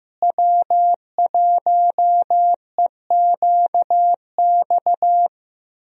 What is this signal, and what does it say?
Morse code, heard as a keyed tone, about 15 wpm, 700 Hz. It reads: W1EQX
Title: USA Call Signs - 5 Characters - Repeated 3x - 15wpm